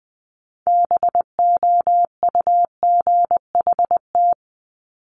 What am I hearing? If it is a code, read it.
Morse code, heard as a keyed tone, about 20 wpm, 700 Hz. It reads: BOUGHT
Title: Multi-letter Phonemes Repeated 3x 20wpm